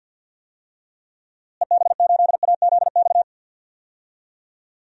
Morse code, heard as a keyed tone, about 52 wpm, 700 Hz. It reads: EB8AZX
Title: DXCC-100 Call Signs - 6 Characters - Repeated 2x - 50wpm